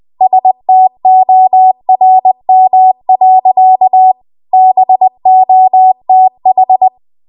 Morse code, heard as a keyed tone, about 20 wpm, 750 Hz. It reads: STORM. BOTH
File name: Archive / 20 WPM / 20WPM